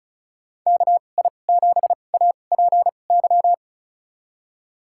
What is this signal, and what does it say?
Morse code, heard as a keyed tone, about 36 wpm, spaced as for 25 wpm, 700 Hz. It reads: KI7APY
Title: USA Call Signs - 2x Character Spacing - 35wpm